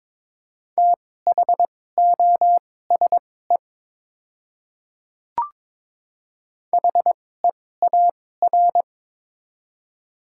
Morse code, heard as a keyed tone, about 21 wpm, spaced as for 16 wpm, 700 Hz. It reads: THOSE HEAR
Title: 2x Character Spacing - Top 300 Words - 22wpm